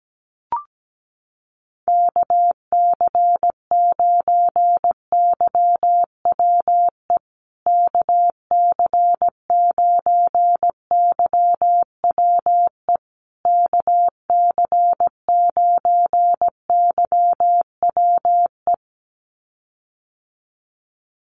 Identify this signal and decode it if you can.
Morse code, heard as a keyed tone, about 17 wpm, 700 Hz. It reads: KC9YWE KC9YWE KC9YWE
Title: USA Call Signs Repeated 3x 17wpm